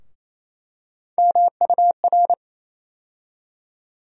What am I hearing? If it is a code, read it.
Morse code, heard as a keyed tone, about 28 wpm, 700 Hz. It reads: MUR
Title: USA Call Sign Postfixes - 3 Letters 28wpm